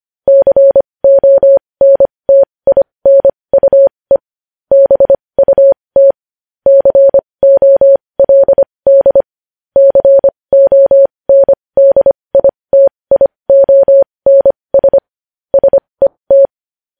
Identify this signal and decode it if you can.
Morse code, heard as a keyed tone, about 25 wpm, spaced as for 20 wpm, 550 Hz. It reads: CONTINUE BUT COLD CONDITIONS SET
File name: podcast / News headlines-1x-25x20